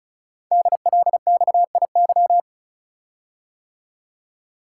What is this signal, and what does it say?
Morse code, heard as a keyed tone, about 36 wpm, 690 Hz. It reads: DLXIY
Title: Code Groups - 5 Letters - 35wpm